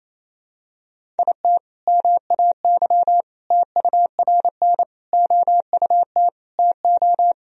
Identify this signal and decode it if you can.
Morse code, heard as a keyed tone, about 28 wpm, 700 Hz. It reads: IT MAY TURN OUT TO
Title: Sentences from Top 400 Words and Abbreviations 28wpm